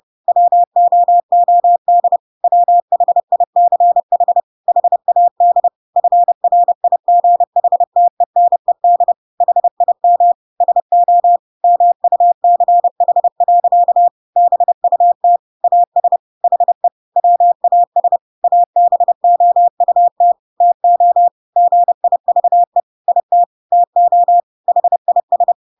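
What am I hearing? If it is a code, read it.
Morse code, heard as a keyed tone, about 30 wpm, 700 Hz. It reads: WOOD WHICH HAD FRIGHTENED HIM SO MUCH. BUT AS HE WAS ABOUT TO GIVE IT TO HIS
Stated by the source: Chapter 2 of The Adventures of Pinocchio read as a book in Morse code (CW) at 30 wpm, by ik1hge.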